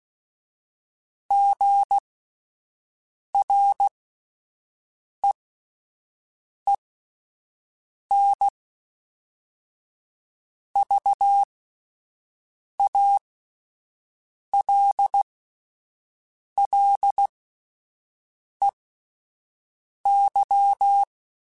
Morse code, heard as a keyed tone, about 16 wpm, spaced as for 5 wpm, 800 Hz. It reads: GREENVALLEY
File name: fromSMs / FromSMs mp3s / Cities@5